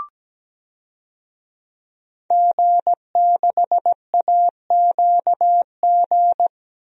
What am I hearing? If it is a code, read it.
Morse code, heard as a keyed tone, about 17 wpm, 700 Hz. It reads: G6AQG
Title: DXCC-100 Call Signs - 5 Characters - Repeated 3x - 17wpm